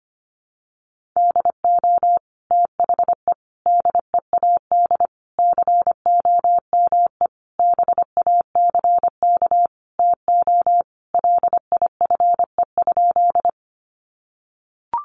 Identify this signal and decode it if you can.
Morse code, heard as a keyed tone, about 25 wpm, 700 Hz. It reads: DO THE DEAD COME BACK TO LIFE?
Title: Sentences from Top 1500 Words 25wpm